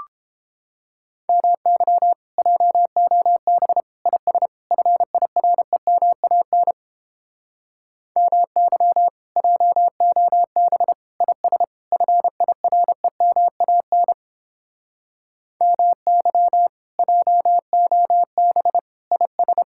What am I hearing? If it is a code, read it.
Morse code, heard as a keyed tone, about 29 wpm, 700 Hz. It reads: MY JOB IS FIREMAN MY JOB IS FIREMAN MY JOB IS